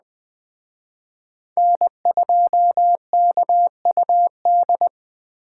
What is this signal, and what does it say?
Morse code, heard as a keyed tone, about 21 wpm, 700 Hz. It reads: N2KUD
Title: Speed Racing - USA 1x3 Call Signs 17wpm